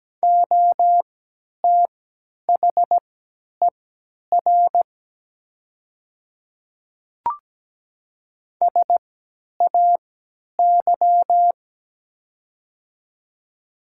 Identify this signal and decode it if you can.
Morse code, heard as a keyed tone, about 17 wpm, spaced as for 10 wpm, 700 Hz. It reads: OTHER SAY
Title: 3x Character Spacing - Top 300 Words - 17wpm